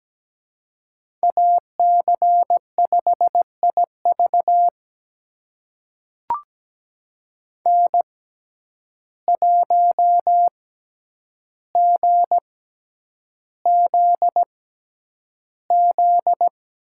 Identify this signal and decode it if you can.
Morse code, heard as a keyed tone, about 17 wpm, spaced as for 9 wpm, 700 Hz. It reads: AC5IV N1GZZ
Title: USA Call Signs - 5 Characters - 6x Character Spacing - 17wpm